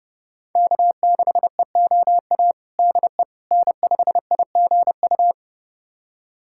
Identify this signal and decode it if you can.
Morse code, heard as a keyed tone, about 32 wpm, 700 Hz. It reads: K6EOA DE N5IGU
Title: USA Callsign DE Callsign - 5 Characters - 30wpm